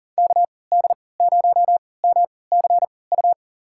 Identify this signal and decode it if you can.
Morse code, heard as a keyed tone, about 40 wpm, spaced as for 23 wpm, 700 Hz. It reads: KD0MCU
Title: USA Call Signs - 6 Characters - 3x Character Spacing - 40wpm